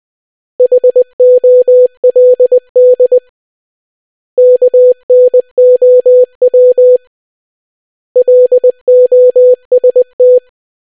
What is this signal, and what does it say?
Morse code, heard as a keyed tone, about 20 wpm, 500 Hz. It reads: HOLD KNOW LOST